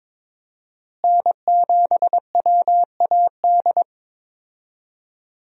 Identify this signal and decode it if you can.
Morse code, heard as a keyed tone, about 22 wpm, 700 Hz. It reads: N7WAD
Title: USA Call Signs - 5 Characters - Repeated 2x - 22wpm